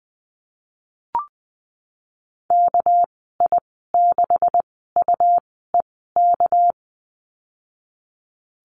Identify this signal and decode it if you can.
Morse code, heard as a keyed tone, about 20 wpm, spaced as for 15 wpm, 700 Hz. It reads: KI6UEK
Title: 231 Lesson 37g - Letter x - Call Signs 20wpm